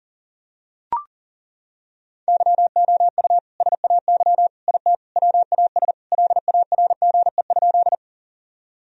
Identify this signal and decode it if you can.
Morse code, heard as a keyed tone, about 37 wpm, 700 Hz. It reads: YOU SAY IT WAS LARGE?
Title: Sentences from Top 1000 Words - Review Part 1 - 40wpm